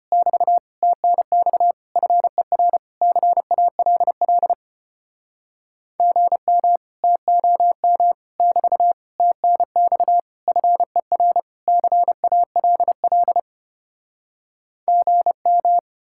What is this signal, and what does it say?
Morse code, heard as a keyed tone, about 33 wpm, 700 Hz. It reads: <BT> TNX FER CALL GM TOM <BT> TNX FER CALL GM